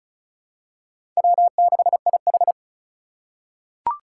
Morse code, heard as a keyed tone, about 34 wpm, 700 Hz. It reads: W6IH
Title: USA Call Signs - 4 Characters - 4x Character Spacing - 35wpm